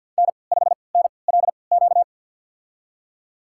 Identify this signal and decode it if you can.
Morse code, heard as a keyed tone, about 50 wpm, spaced as for 28 wpm, 700 Hz. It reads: N5NLQ
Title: USA Call Signs - 5 Characters - 3x Character Spacing - 50wpm